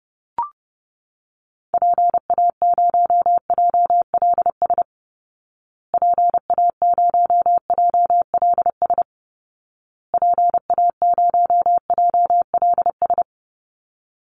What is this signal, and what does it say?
Morse code, heard as a keyed tone, about 30 wpm, 700 Hz. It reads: PA0JLS PA0JLS PA0JLS
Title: Call Signs Repeated 2x - 30wpm